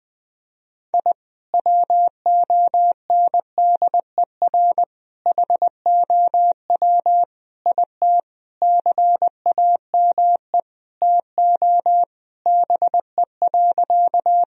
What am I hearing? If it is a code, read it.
Morse code, heard as a keyed tone, about 20 wpm, 700 Hz. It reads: I WONDER HOW IT CAME TO BE.
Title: Sentences from Top 1000 Words - Review Part 3 - 20wpm